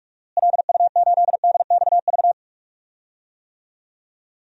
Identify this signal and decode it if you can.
Morse code, heard as a keyed tone, about 42 wpm, 700 Hz. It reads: LU8DXV
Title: DXCC-100 Call Signs - 6 Characters - Repeated 2x - 45wpm